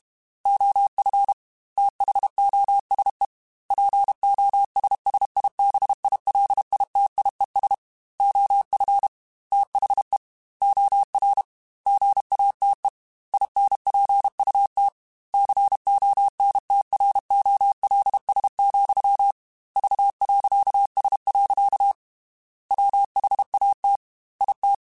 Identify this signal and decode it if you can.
Morse code, heard as a keyed tone, about 32 wpm, 800 Hz. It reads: OF THOSE POSSIBILITIES OF THE OR GATE INPUT CONTROLS, V.S. WHAT IT
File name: Technical Subjects / Technical Subjects-mp3s / OR Gate @30